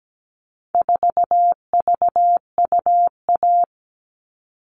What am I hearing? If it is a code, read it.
Morse code, heard as a keyed tone, about 17 wpm, 700 Hz. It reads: 4VUA
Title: All Cut Numbers - 4-Digit 17wpm